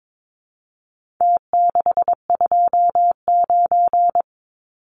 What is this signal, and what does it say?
Morse code, heard as a keyed tone, about 22 wpm, 700 Hz. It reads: T629
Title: Numbers - Contest Style - 4-Digit 22wpm